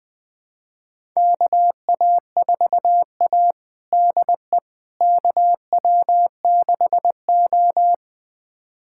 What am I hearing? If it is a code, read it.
Morse code, heard as a keyed tone, about 20 wpm, 700 Hz. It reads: KA4A DE KW6O
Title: USA Callsign DE Callsign - 4 Characters - 20wpm